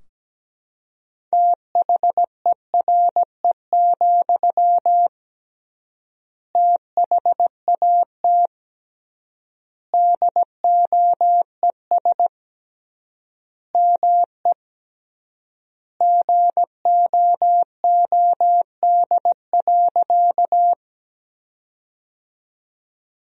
Morse code, heard as a keyed tone, about 17 wpm, 700 Hz. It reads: THERE, THAT DOES ME GOOD.
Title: 3x Word-Spacing - Sentences from Top 200 Words - 17wpm